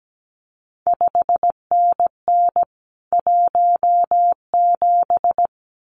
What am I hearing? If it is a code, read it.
Morse code, heard as a keyed tone, about 17 wpm, 700 Hz. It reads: 5NN 17
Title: CQ WW DX Contest Exchange Encore - 17wpm